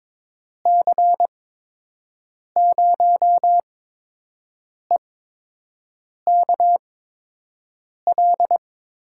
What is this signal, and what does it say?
Morse code, heard as a keyed tone, about 22 wpm, spaced as for 6 wpm, 700 Hz. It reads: C0EKL
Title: USA Call Signs - 8x Character Spacing - 22wpm